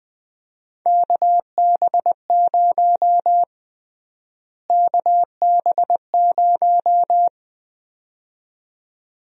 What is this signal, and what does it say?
Morse code, heard as a keyed tone, about 20 wpm, 700 Hz. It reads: KB0 KB0
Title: USA Call Sign Prefixes - Repeated 2x - 20wpm